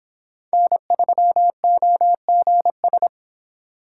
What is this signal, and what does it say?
Morse code, heard as a keyed tone, about 26 wpm, 700 Hz. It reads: N3OGS